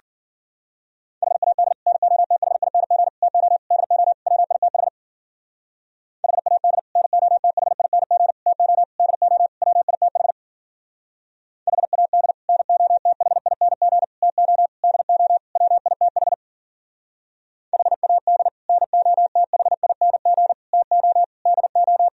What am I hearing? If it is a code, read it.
Morse code, heard as a keyed tone, about 50 wpm, 700 Hz. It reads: HAD NOTHING TO DO WITH HAD NOTHING TO DO WITH HAD NOTHING TO DO WITH HAD NOTHING TO DO
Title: Speed-Racing - Sets of 5 Words 40wpm